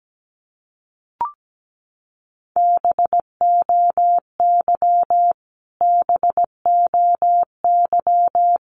Morse code, heard as a keyed tone, about 17 wpm, 700 Hz. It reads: BOY BOY
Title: Top 300 Words Repeated 17wpm